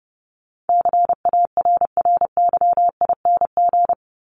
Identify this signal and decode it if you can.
Morse code, heard as a keyed tone, about 30 wpm, 700 Hz. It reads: CARRYING
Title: ING Suffix - Top 100 Words with Base Words 30wpm